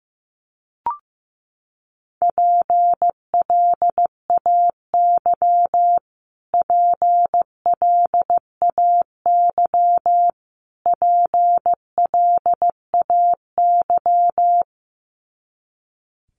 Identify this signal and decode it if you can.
Morse code, heard as a keyed tone, about 15 wpm, 700 Hz. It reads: PLAY PLAY PLAY
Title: Top 300 Words Repeated 15wpm